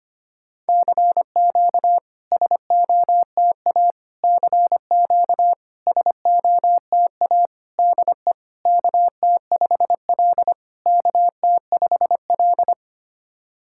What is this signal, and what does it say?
Morse code, heard as a keyed tone, about 25 wpm, 700 Hz. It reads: CQ SOTA CQ SOTA DE KT5L KT5L
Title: SOTA QSO Practice 25wpm